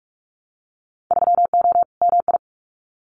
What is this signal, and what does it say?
Morse code, heard as a keyed tone, about 44 wpm, 700 Hz. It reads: 3O MI
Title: Winter Field Day Exchanges 45wpm